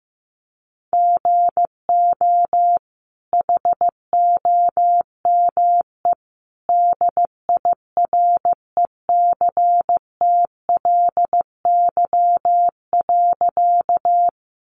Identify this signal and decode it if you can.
Morse code, heard as a keyed tone, about 15 wpm, 700 Hz. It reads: GO HOME DIRECTLY.
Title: Sentences from Top 800 Words 15wpm